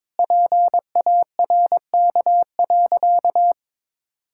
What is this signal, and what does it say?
Morse code, heard as a keyed tone, about 22 wpm, 700 Hz. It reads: PARK.
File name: practice / 22wpm / Sentences with 4-Word Phrases - 22wpm